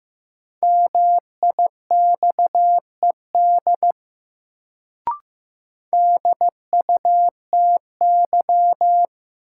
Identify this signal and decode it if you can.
Morse code, heard as a keyed tone, about 15 wpm, 700 Hz. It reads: MIXED DUTY
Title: Sentences from Top 1500 Words - New Word Drill - 15wpm